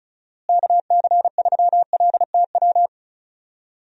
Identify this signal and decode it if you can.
Morse code, heard as a keyed tone, about 36 wpm, 700 Hz. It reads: KC3LTW
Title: USA Call Signs - 6 Characters - Repeated 3x - 35wpm